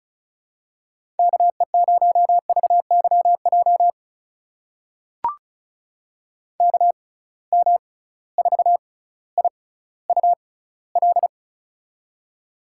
Morse code, heard as a keyed tone, about 36 wpm, spaced as for 18 wpm, 700 Hz. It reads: KE0VYJ KM4IUL
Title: USA Call Signs - 6 Characters - 6x Character Spacing - 35wpm